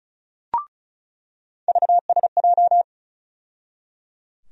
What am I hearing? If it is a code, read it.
Morse code, heard as a keyed tone, about 36 wpm, 700 Hz. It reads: VSJ
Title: USA Call Sign Postfixes - 3 Letters 35wpm